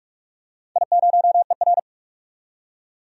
Morse code, heard as a keyed tone, about 44 wpm, 700 Hz. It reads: I0ER
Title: DXCC-100 Call Signs - 4 Characters - 45wpm